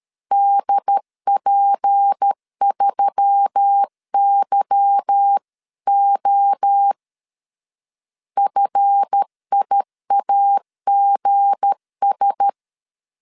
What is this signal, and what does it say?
Morse code, heard as a keyed tone, about 13 wpm, 800 Hz. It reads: DP3YO FIAGS
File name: LEZIONI / 8 YJ45 / 8 Y 12wpm 2228